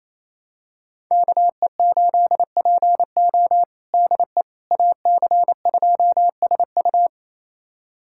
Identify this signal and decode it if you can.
Morse code, heard as a keyed tone, about 28 wpm, 700 Hz. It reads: KE8PO DE AC2SU
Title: USA Callsign DE Callsign - 5 Characters - 28wpm